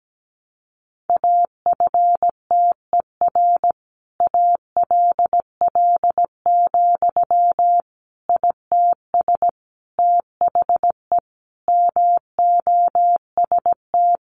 Morse code, heard as a keyed tone, about 17 wpm, 700 Hz. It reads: AFTER ALL, ITS THE MOST
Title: Sentences from Top 700 Words 17wpm